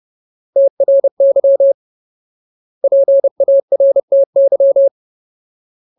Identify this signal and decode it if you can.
Morse code, heard as a keyed tone, about 30 wpm, 550 Hz. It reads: TRY PARTY
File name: Ultimate Word List / 30WPM Ultimate